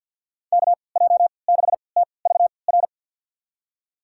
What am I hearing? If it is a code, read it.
Morse code, heard as a keyed tone, about 53 wpm, spaced as for 29 wpm, 700 Hz. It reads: KJ6TVR
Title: USA Call Signs - 6 Characters - 3x Character Spacing - 50wpm